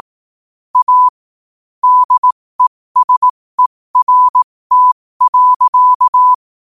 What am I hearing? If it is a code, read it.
Morse code, heard as a keyed tone, about 18 wpm, spaced as for 15 wpm, 1000 Hz. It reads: A DESERT.
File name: qotdfiles / 15wpm